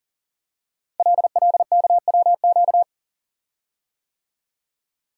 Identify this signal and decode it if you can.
Morse code, heard as a keyed tone, about 42 wpm, 690 Hz. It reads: LLKWQ
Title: Code Groups - 5 Letter:Numbers - 40wpm